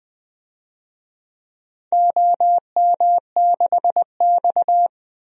Morse code, heard as a keyed tone, about 20 wpm, 700 Hz. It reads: OM6X
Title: DXCC-100 Call Signs - 4 Characters - Repeated 3x - 20wpm